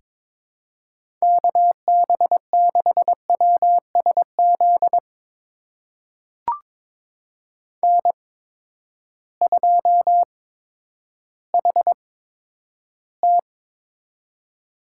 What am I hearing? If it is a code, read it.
Morse code, heard as a keyed tone, about 22 wpm, spaced as for 11 wpm, 700 Hz. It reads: KB6WSZ N2HT
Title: USA Call Signs - 8x Character Spacing - 22wpm